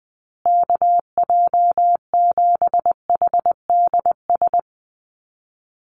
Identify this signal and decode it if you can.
Morse code, heard as a keyed tone, about 20 wpm, 700 Hz. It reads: KJ7HDS
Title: USA Call Signs Repeated 3x 20wpm